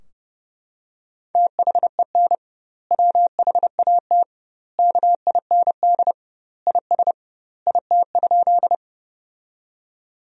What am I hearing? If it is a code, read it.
Morse code, heard as a keyed tone, about 30 wpm, 700 Hz. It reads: THEN WHAT KIND IS IT?
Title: 2x Word-Spacing - Sentences from Top 200 Words - 30wpm